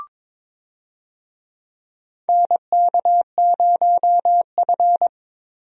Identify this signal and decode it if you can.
Morse code, heard as a keyed tone, about 22 wpm, 700 Hz. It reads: NK0F
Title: USA Call Signs - 4 Characters - Repeated 3x - 22wpm